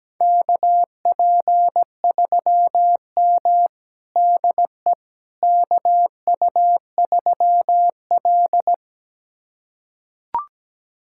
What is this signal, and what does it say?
Morse code, heard as a keyed tone, about 17 wpm, 700 Hz. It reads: KP3M DE KU3L
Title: USA Callsign DE Callsign - 4 Characters - 17wpm